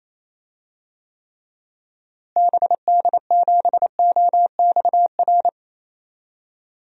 Morse code, heard as a keyed tone, about 28 wpm, 700 Hz. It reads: BD7OXR
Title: DXCC-100 Call Signs - 6 Characters - Repeated 2x - 28wpm